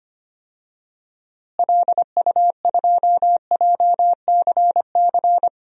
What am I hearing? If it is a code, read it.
Morse code, heard as a keyed tone, about 25 wpm, 700 Hz. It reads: LU2JCC
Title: DXCC-100 Call Signs - 6 Characters - Repeated 2x - 25wpm